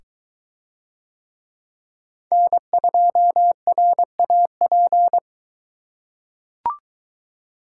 Morse code, heard as a keyed tone, about 21 wpm, 700 Hz. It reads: N2RAP